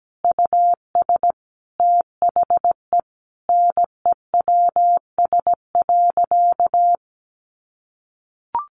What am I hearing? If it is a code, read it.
Morse code, heard as a keyed tone, about 17 wpm, 700 Hz. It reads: US THE NEWS.
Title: Sentences from Top 800 Words 17wpm